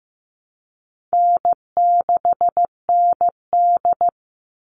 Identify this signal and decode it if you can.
Morse code, heard as a keyed tone, about 15 wpm, 700 Hz. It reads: N6ND
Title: All Cut Numbers - 4-Digit 15wpm